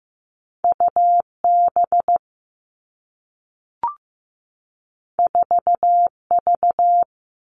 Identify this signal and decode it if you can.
Morse code, heard as a keyed tone, about 15 wpm, 700 Hz. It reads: UB 4V
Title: Two Challenging Letter-Number Combinations 15wpm